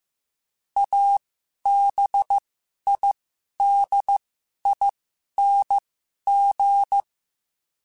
Morse code, heard as a keyed tone, about 15 wpm, spaced as for 11 wpm, 800 Hz. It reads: ABIDING
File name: Special INGs-mp3s / Special INGs-mp3s / Sngl Ing @10